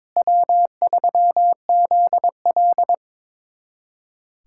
Morse code, heard as a keyed tone, about 22 wpm, 700 Hz. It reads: W3ZL
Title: USA Callsign DE Callsign - 4 Characters - 22wpm